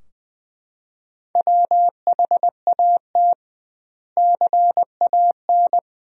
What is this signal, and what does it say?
Morse code, heard as a keyed tone, about 21 wpm, 700 Hz. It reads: WHAT CAN
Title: 2x Word-Spacing - Sentences from Top 100 Words - 20wpm